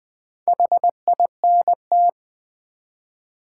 Morse code, heard as a keyed tone, about 20 wpm, 700 Hz. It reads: HINT
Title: Build A Word - 1 Letter At A Time - 3 to 5-Letter Words - 20wpm